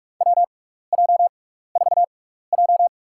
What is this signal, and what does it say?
Morse code, heard as a keyed tone, about 44 wpm, spaced as for 15 wpm, 700 Hz. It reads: WJ4J
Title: USA Call Signs - 4 Characters - 6x Character Spacing - 45wpm